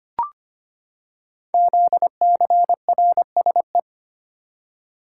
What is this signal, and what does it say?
Morse code, heard as a keyed tone, about 25 wpm, 700 Hz. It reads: ZCRSE
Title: Code Groups - 5 Letters - 25wpm